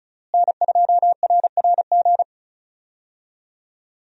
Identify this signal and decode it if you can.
Morse code, heard as a keyed tone, about 36 wpm, 700 Hz. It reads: N2RRG
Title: USA Call Signs - 5 Characters - Repeated 3x - 35wpm